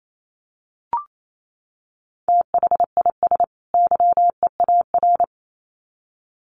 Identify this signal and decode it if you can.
Morse code, heard as a keyed tone, about 28 wpm, 700 Hz. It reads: THIS YEAR
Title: Sets of 2 Words 28wpm